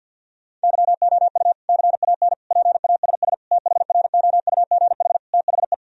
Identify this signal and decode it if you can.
Morse code, heard as a keyed tone, about 50 wpm, 700 Hz. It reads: YOU CAN PASS THROUGH THE
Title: Sentences from Top 1000 Words - Review Part 1 - 50wpm